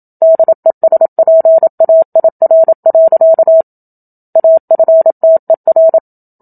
Morse code, heard as a keyed tone, about 27 wpm, 650 Hz. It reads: DESPAIR. AFTER
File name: Book Two / 01-Chapter 1 - 27 WPM